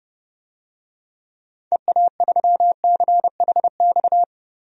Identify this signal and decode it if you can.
Morse code, heard as a keyed tone, about 32 wpm, 700 Hz. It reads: EA3CHX
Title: DXCC-100 Call Signs - 6 Characters - Repeated 2x - 30wpm